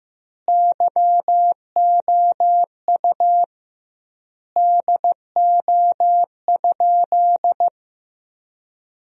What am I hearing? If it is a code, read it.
Morse code, heard as a keyed tone, about 15 wpm, 700 Hz. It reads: YOU DO?
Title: 2x Word-Spacing - Sentences from Top 100 Words - 15wpm